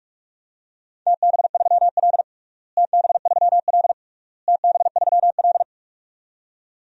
Morse code, heard as a keyed tone, about 46 wpm, 700 Hz. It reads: TB3L TB3L TB3L
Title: DXCC-100 Call Signs - 4 Characters - Repeated 3x - 45wpm